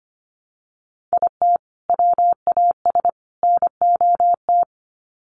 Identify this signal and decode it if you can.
Morse code, heard as a keyed tone, about 25 wpm, 700 Hz. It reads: IT WAS NOT
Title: Sets of 3 Words 25wpm